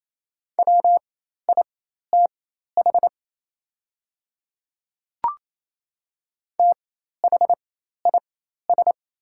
Morse code, heard as a keyed tone, about 28 wpm, spaced as for 13 wpm, 700 Hz. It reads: WITH THIS
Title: 4x Character Spacing - Top 100 Words - 28wpm